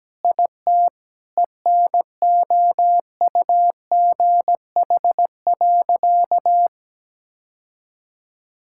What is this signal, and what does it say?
Morse code, heard as a keyed tone, about 17 wpm, 700 Hz. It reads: IT ENOUGH.
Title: Sentences from Top 1000 Words - Review Part 6 - 17wpm